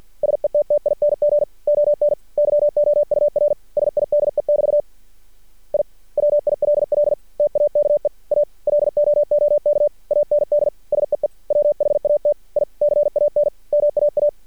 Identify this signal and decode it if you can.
Morse code, heard as a keyed tone, about 50 wpm, 590 Hz. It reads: SETTING ON YOUR SIDE<BT> I WILL TAKE A LOOK AND SEE WHAT I CAN MAA
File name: archived / CRSnet